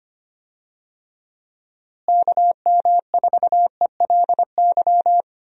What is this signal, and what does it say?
Morse code, heard as a keyed tone, about 25 wpm, 700 Hz. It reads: KM4ELY
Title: USA Call Signs - 6 Characters - Repeated 2x - 25wpm